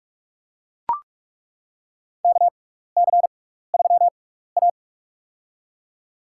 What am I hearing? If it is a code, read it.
Morse code, heard as a keyed tone, about 42 wpm, spaced as for 15 wpm, 700 Hz. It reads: KC3A
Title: USA Call Signs - 4 Characters - 6x Character Spacing - 45wpm